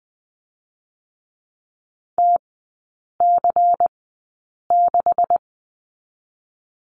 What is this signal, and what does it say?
Morse code, heard as a keyed tone, about 20 wpm, spaced as for 8 wpm, 700 Hz. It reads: TC6
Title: 127 Lesson 21e - Number 6 - Three Characters 20wpm